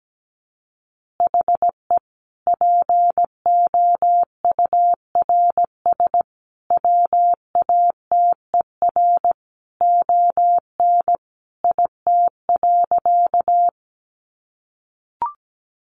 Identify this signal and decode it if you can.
Morse code, heard as a keyed tone, about 17 wpm, 700 Hz. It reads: HE POURS WATER ON IT.
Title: Additional Easy Phrases 17wpm